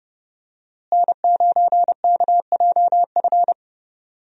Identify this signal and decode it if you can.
Morse code, heard as a keyed tone, about 32 wpm, 700 Hz. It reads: N9KJF
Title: USA Call Signs - 5 Characters - Repeated 3x - 30wpm